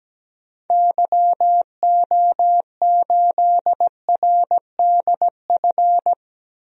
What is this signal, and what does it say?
Morse code, heard as a keyed tone, about 17 wpm, 700 Hz. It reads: YO8RDF
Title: DXCC-100 Call Signs - 6 Characters - Repeated 2x - 17wpm